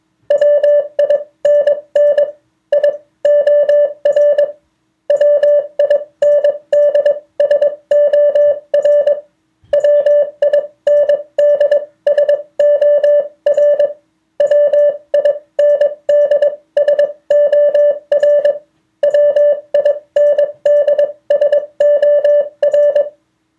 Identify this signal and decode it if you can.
Morse code, heard as a keyed tone, about 22 wpm, spaced as for 18 wpm, 580 Hz. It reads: WINNIOR WINDSOR WINDSOR WINDSOR WINDSOR